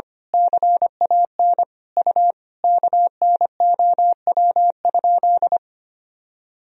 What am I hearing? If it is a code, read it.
Morse code, heard as a keyed tone, about 25 wpm, 700 Hz. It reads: CAN U KNOW?
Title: Sentences from Top 100 Words and Abbreviations 25wpm